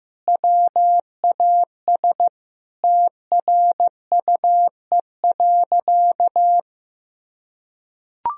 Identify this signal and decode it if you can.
Morse code, heard as a keyed tone, about 15 wpm, 700 Hz. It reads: WAS TRUE.
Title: Sentences from Top 400 Words and Abbreviations 15wpm